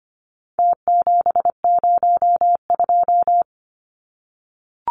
Morse code, heard as a keyed tone, about 25 wpm, 700 Hz. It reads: T702
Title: Numbers - Contest Style - 4-Digit 25wpm